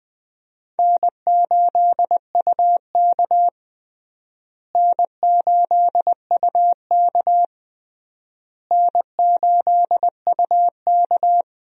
Morse code, heard as a keyed tone, about 20 wpm, 700 Hz. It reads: N8UK N8UK N8UK
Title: USA Call Signs - 4 Characters - Repeated 3x - 20wpm